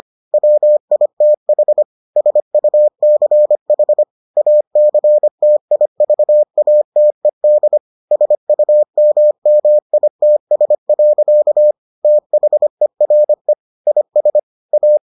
Morse code, heard as a keyed tone, about 25 wpm, 600 Hz. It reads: WITH SUCH ACTIVATED SUMMITS. THERE IS A